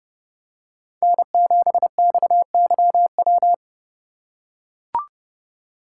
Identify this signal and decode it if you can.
Morse code, heard as a keyed tone, about 30 wpm, 700 Hz. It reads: N7XYW